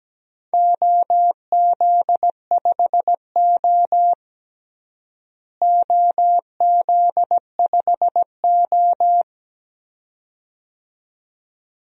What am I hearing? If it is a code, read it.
Morse code, heard as a keyed tone, about 17 wpm, 700 Hz. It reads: OZ5O OZ5O
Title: DXCC-100 Call Signs - 4 Characters - Repeated 2x - 17wpm